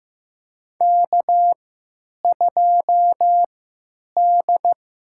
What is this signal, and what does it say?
Morse code, heard as a keyed tone, about 15 wpm, spaced as for 9 wpm, 700 Hz. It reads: K2D
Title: USA Call Signs - 3x Character Spacing - 15wpm